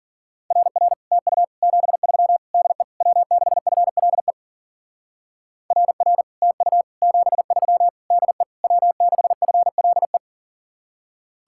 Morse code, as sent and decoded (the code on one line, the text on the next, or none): .-. .-. / - ..- / --... ...-- / -.. / .-- -.... ..-. .-.. / .-. .-. / - ..- / --... ...-- / -.. / .-- -.... ..-. .-.. .
RR TU 73 DE W6FLE RR TU 73 DE W6FLE